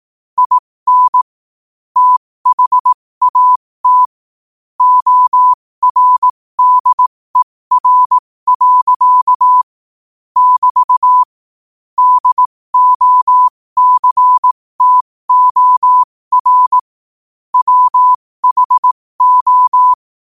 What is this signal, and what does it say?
Morse code, heard as a keyed tone, about 18 wpm, spaced as for 15 wpm, 1000 Hz. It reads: IN THAT ORDER. <BT> DOCTOR WHO
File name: qotdfiles / 15wpm